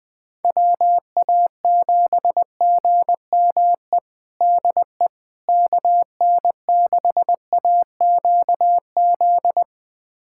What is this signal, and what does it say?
Morse code, heard as a keyed tone, about 20 wpm, 700 Hz. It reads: WA7GME DE KN6AQZ
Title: USA Callsign DE Callsign 20wpm